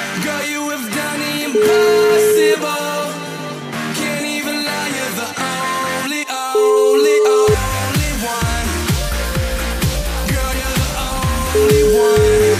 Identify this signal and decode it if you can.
Ringback tone (Europe, 425 Hz), with noise.